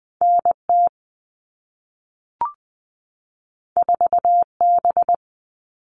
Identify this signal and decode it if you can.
Morse code, heard as a keyed tone, about 20 wpm, 700 Hz. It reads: NT 4B
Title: All Cut Numbers - 2-Digit 20wpm